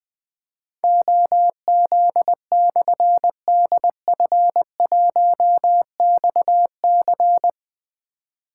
Morse code, heard as a keyed tone, about 21 wpm, 700 Hz. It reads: OZ/DF1XC
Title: Difficult Call Signs 20wpm